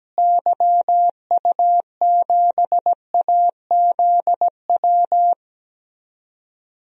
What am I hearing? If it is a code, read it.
Morse code, heard as a keyed tone, about 17 wpm, 700 Hz. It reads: YU7AZW
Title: DXCC-100 Call Signs - 6 Characters - Repeated 3x - 17wpm